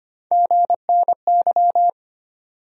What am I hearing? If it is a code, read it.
Morse code, heard as a keyed tone, about 25 wpm, 700 Hz. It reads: GNY
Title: USA Call Sign Postfixes - 3 Letters 25wpm